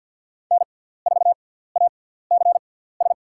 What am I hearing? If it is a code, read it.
Morse code, heard as a keyed tone, about 52 wpm, spaced as for 17 wpm, 700 Hz. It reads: N4ACS
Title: USA Call Signs - 6x Character Spacing - 50wpm